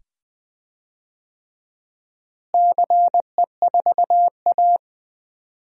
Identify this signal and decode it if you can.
Morse code, heard as a keyed tone, about 21 wpm, 700 Hz. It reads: CE4A
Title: DXCC-100 Call Signs - 4 Characters - Repeated 2x - 20wpm